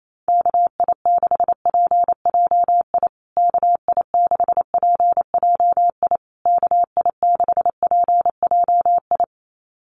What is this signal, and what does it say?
Morse code, heard as a keyed tone, about 28 wpm, 690 Hz. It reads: KI6PJI KI6PJI KI6PJI
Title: USA Call Signs Repeated 3x 28wpm